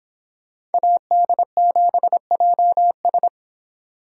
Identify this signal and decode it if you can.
Morse code, heard as a keyed tone, about 29 wpm, 700 Hz. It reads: AD7JS